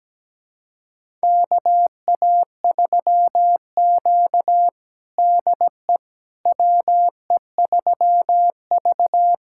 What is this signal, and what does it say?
Morse code, heard as a keyed tone, about 17 wpm, 700 Hz. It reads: KA3Q DE WE3V
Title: USA Callsign DE Callsign - 4 Characters - 17wpm